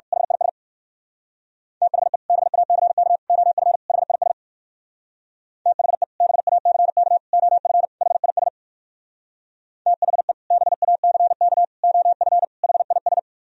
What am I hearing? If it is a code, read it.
Morse code, heard as a keyed tone, about 56 wpm, 700 Hz. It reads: HIS THE BACK OF HIS THE BACK OF HIS THE BACK OF HIS
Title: Speed-Racing - Sets of 4 Words 45wpm